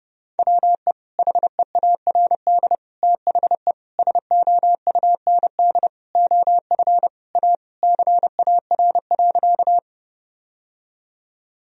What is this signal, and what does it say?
Morse code, heard as a keyed tone, about 31 wpm, 700 Hz. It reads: WE HEARD THE SOUND OF A CAR.
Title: Sentences from Top 1000 Words - Review Part 1 - 30wpm